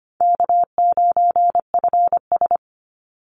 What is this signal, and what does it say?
Morse code, heard as a keyed tone, about 25 wpm, 700 Hz. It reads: K9FS
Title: Easy Call Signs - 25wpm